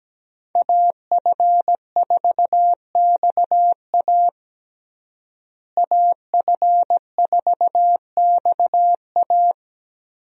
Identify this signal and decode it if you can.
Morse code, heard as a keyed tone, about 17 wpm, 700 Hz. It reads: AF4XA AF4XA
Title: USA Call Signs - 5 Characters - Repeated 2x - 17wpm